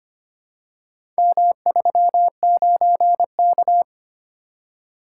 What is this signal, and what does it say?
Morse code, heard as a keyed tone, about 25 wpm, 700 Hz. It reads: M39K
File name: practice / 25wpm / Four Letter-Number Combinations - 25wpm